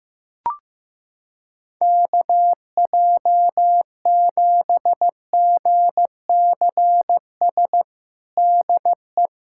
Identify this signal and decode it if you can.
Morse code, heard as a keyed tone, about 15 wpm, 700 Hz. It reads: KJ7GCS DE
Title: USA Callsign DE Callsign 15wpm